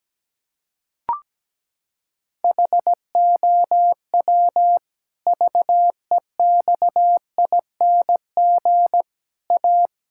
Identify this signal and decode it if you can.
Morse code, heard as a keyed tone, about 17 wpm, 700 Hz. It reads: HOW VEXING A
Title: Pangram Challenge v2 - 17wpm